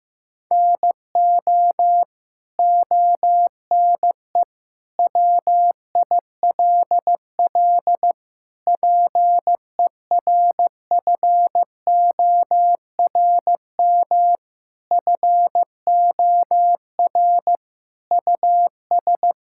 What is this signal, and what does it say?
Morse code, heard as a keyed tone, about 16 wpm, 700 Hz. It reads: NO ONE WILL PERFORM FOR US
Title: Sentences from Top 1000 Words - Review Part 5 - 15wpm